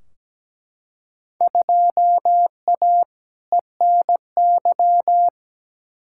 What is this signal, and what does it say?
Morse code, heard as a keyed tone, about 17 wpm, 700 Hz. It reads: 2A ENY
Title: Field Day Exchanges Encore - 17wpm